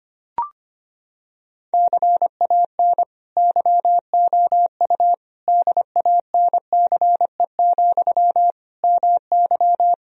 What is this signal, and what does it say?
Morse code, heard as a keyed tone, about 25 wpm, 700 Hz. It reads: CAN YOU DANCE, MY
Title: Sentences from Top 1000 Words - Review Part 4 - 25wpm